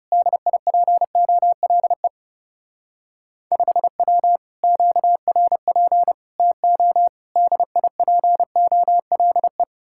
Morse code, heard as a keyed tone, about 35 wpm, 700 Hz. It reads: DIPOLE 5W QRP TO DIPOLE
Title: Speed-Racing Common QSO Phrases - 30wpm